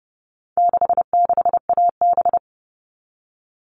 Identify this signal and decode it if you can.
Morse code, heard as a keyed tone, about 30 wpm, 700 Hz. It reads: 66AB
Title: All Cut Numbers - 4-Digit 30wpm